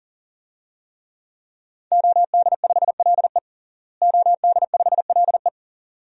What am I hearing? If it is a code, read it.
Morse code, heard as a keyed tone, about 40 wpm, 700 Hz. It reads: OD5LE OD5LE
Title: DXCC-100 Call Signs - 5 Characters - Repeated 2x - 40wpm